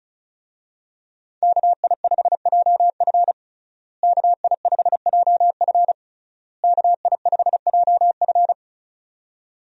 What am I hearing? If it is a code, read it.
Morse code, heard as a keyed tone, about 34 wpm, 700 Hz. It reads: KI5JF KI5JF KI5JF
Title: USA Call Signs - 5 Characters - Repeated 3x - 35wpm